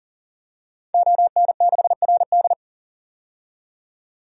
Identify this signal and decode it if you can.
Morse code, heard as a keyed tone, about 40 wpm, 700 Hz. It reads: ON6RD
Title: DXCC-100 Call Signs - 5 Characters - 40wpm